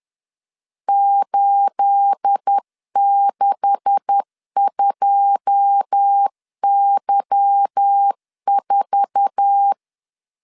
Morse code, heard as a keyed tone, about 10 wpm, 800 Hz. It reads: 862Y4
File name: LEZIONI / 9 286 10wpm 2735